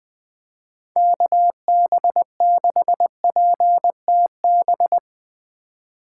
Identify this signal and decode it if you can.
Morse code, heard as a keyed tone, about 22 wpm, 700 Hz. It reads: KB6PTB
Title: Speed Racing - USA 2x3 Call Signs 20wpm